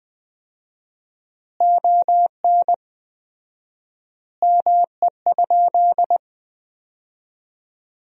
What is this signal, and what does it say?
Morse code, heard as a keyed tone, about 20 wpm, 700 Hz. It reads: ON ME?
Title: 4x Word-Spacing - Sentences from Top 500 Words - 20wpm